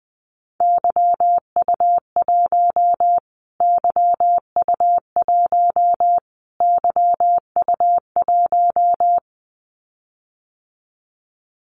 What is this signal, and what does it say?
Morse code, heard as a keyed tone, about 20 wpm, 700 Hz. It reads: YU1 YU1 YU1
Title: Callsign Prefixes for Top 25 DXCC - Repeated 3x - 20wpm